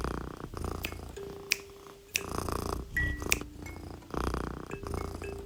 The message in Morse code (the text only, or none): none